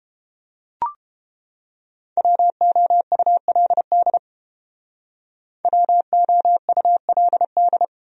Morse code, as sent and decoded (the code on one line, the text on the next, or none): .-- --- ..- .-.. -.. / .-- --- ..- .-.. -..
WOULD WOULD